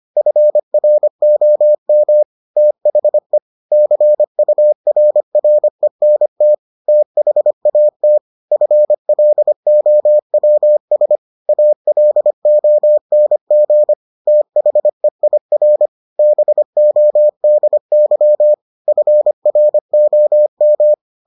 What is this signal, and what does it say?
Morse code, heard as a keyed tone, about 25 wpm, 600 Hz. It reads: FROM THE CURRENT THAT FLOWS ALONG THEIR BODY FROM